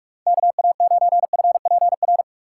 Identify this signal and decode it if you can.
Morse code, heard as a keyed tone, about 46 wpm, 700 Hz. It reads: KA9FPR
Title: USA Call Signs - 6 Characters - 3x Character Spacing - 45wpm